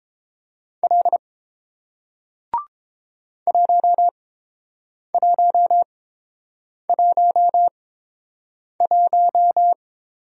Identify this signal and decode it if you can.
Morse code, heard as a keyed tone, about 30 wpm, spaced as for 7 wpm, 700 Hz. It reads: L 1111